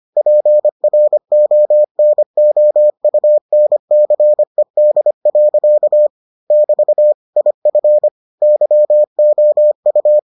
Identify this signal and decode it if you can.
Morse code, heard as a keyed tone, about 25 wpm, 600 Hz. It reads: PRONOUNCED. <BT> IF YOU